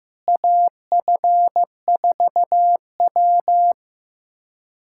Morse code, heard as a keyed tone, about 15 wpm, 700 Hz. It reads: AF4W
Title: USA Callsign DE Callsign - 4 Characters - 15wpm